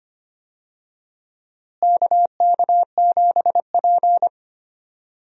Morse code, heard as a keyed tone, about 25 wpm, 700 Hz. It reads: KK7P
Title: USA Call Signs - 4 Characters - Repeated 2x - 25wpm